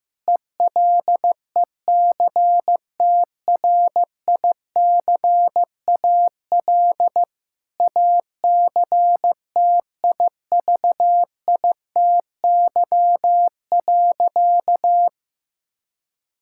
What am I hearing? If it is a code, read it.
Morse code, heard as a keyed tone, about 15 wpm, 700 Hz. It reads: ELECTRICAL ACTIVITY.
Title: Sentences from Top 1000 Words - Review Part 6 - 15wpm